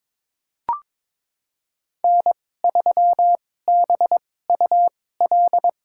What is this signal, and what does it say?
Morse code, heard as a keyed tone, about 22 wpm, spaced as for 16 wpm, 700 Hz. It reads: N3BUL
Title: USA Call Signs - 2x Character Spacing - 22wpm